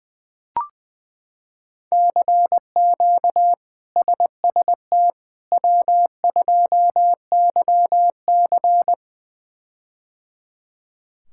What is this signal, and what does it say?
Morse code, heard as a keyed tone, about 20 wpm, 700 Hz. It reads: CQ SST W2YC
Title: K1USN SST QSO Practice - 20wpm